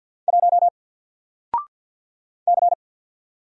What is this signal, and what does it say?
Morse code, heard as a keyed tone, about 48 wpm, 700 Hz. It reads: 1 C
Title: Single Letter-Number Alt 50wpm